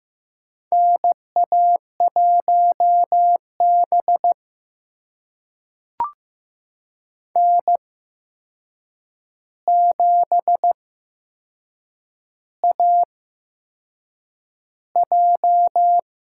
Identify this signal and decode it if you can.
Morse code, heard as a keyed tone, about 15 wpm, spaced as for 6 wpm, 700 Hz. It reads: NA1B N7AJ
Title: USA Call Signs - 4 Characters - 8x Character Spacing - 15wpm